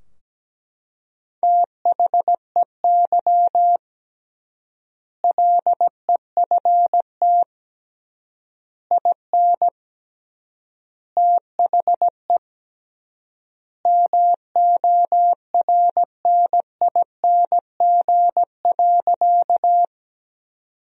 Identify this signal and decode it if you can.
Morse code, heard as a keyed tone, about 17 wpm, 700 Hz. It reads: THEY LEFT IN THE MORNING.
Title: 3x Word-Spacing - Sentences from Top 400 Words - 17wpm